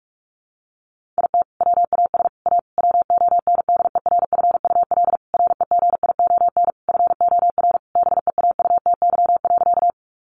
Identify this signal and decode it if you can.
Morse code, heard as a keyed tone, about 44 wpm, 700 Hz. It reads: IT WAS A WONDERFUL REGION FOR BEAUTY.
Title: Sentences from Top 1500 Words 45wpm